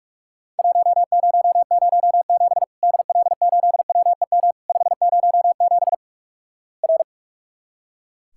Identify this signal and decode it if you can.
Morse code, heard as a keyed tone, about 44 wpm, 700 Hz. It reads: 1007 DL8WEM 507 R
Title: Worked All Europe DX Contest - QTC Exchange - 45wpm